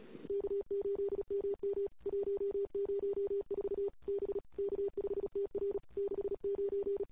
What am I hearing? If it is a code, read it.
Morse code, heard as a keyed tone, about 35 wpm, 410 Hz. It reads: K8MM 104 B K5TR 69
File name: CW / 2008-ss-cw-k5tr-op-n5rz -21:00